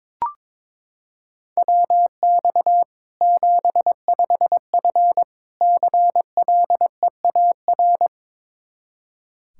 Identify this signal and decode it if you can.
Morse code, heard as a keyed tone, about 22 wpm, 700 Hz. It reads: WX 75F CLEAR
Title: Practice QSO - Exchange 2 - Info - 22wpm